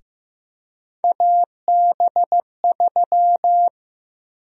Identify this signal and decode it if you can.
Morse code, heard as a keyed tone, about 15 wpm, 700 Hz. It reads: AB3
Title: USA Call Sign Prefixes - Repeated 2x - 15wpm